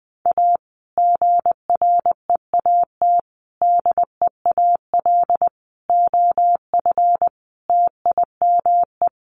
Morse code, heard as a keyed tone, about 20 wpm, 700 Hz. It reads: A GREAT DEAL OF TIME
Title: Sets of 5 Words Encore2 20wpm